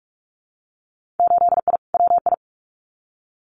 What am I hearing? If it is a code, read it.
Morse code, heard as a keyed tone, about 46 wpm, 700 Hz. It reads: 8I WI
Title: Winter Field Day Exchanges 45wpm